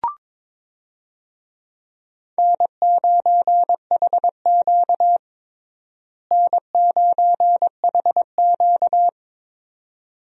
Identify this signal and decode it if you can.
Morse code, heard as a keyed tone, about 22 wpm, 700 Hz. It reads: N9HQ N9HQ
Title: USA Call Signs - 4 Characters - Repeated 2x - 22wpm